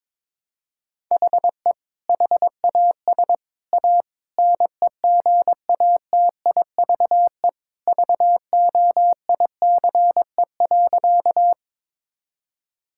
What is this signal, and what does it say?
Morse code, heard as a keyed tone, about 22 wpm, 700 Hz. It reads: HE HAS A NEGATIVE VOICE.
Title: Sentences from Top 1000 Words - Review Part 6 - 22wpm